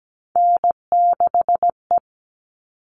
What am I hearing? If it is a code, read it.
Morse code, heard as a keyed tone, about 17 wpm, 700 Hz. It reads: N6E
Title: All Cut Numbers - 3-Digit 17wpm